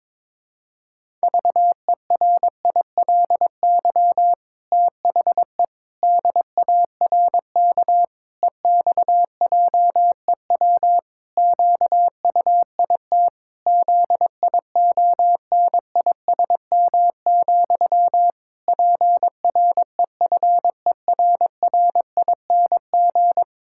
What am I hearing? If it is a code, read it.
Morse code, heard as a keyed tone, about 22 wpm, 700 Hz. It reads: VERILY THE DARK EXJEW QUIT ZIONISM, PREFERRING
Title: Pangram Challenge v2 - 22wpm